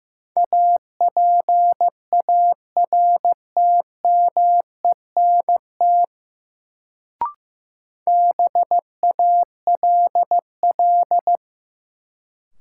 Morse code, heard as a keyed tone, about 15 wpm, 700 Hz. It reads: APARTMENT BALL
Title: Sentences from Top 900 Words - New Word Drill - 15wpm